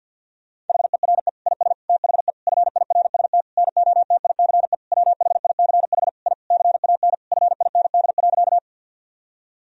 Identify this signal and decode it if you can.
Morse code, heard as a keyed tone, about 50 wpm, 700 Hz. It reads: HERE IS THE FIRST NOTICE WHICH I CAN FIND.
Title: Sentences from Top 1000 Words - Review Part 2 - 50wpm